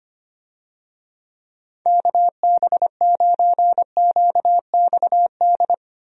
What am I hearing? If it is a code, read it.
Morse code, heard as a keyed tone, about 25 wpm, 700 Hz. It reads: KB9QXD
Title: USA Call Signs - 6 Characters - Repeated 3x - 25wpm